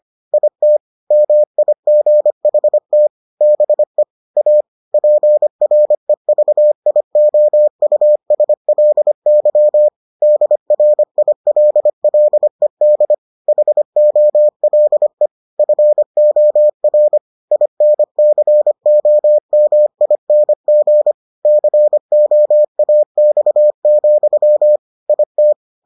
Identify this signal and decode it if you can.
Morse code, heard as a keyed tone, about 25 wpm, 600 Hz. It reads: IT MIGHT BE A PREVIOUSLY DRILLED HOLE FOR INCOMING COAX, IT